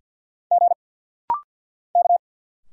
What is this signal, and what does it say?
Morse code, heard as a keyed tone, about 50 wpm, 700 Hz. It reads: G K
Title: Single Letters - Rapid-Fire 50wpm